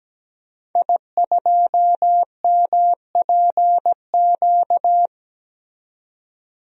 Morse code, heard as a keyed tone, about 17 wpm, 700 Hz. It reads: I2MPQ
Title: DXCC-100 Call Signs - 5 Characters - 17wpm